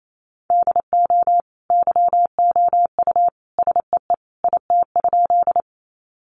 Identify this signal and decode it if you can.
Morse code, heard as a keyed tone, about 28 wpm, 700 Hz. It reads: DO YOU SEE IT?
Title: Sentences from Top 100 Words 28wpm